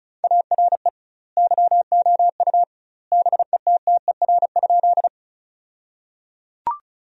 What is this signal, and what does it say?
Morse code, heard as a keyed tone, about 36 wpm, 700 Hz. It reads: ARE YOU BETTER?
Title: 2x Word-Spacing - Sentences from Top 400 Words - 35wpm